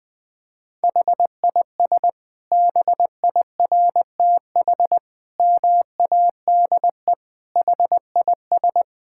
Morse code, heard as a keyed tone, about 21 wpm, 700 Hz. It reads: HIS BIRTH MADE HIS
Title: Sentences from Top 1000 Words - Review Part 5 - 20wpm